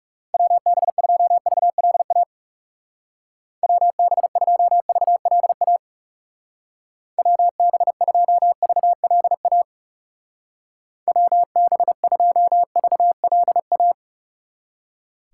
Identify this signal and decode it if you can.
Morse code, heard as a keyed tone, about 39 wpm, 700 Hz. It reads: WB2VLA WB2VLA WB2VLA WB2VLA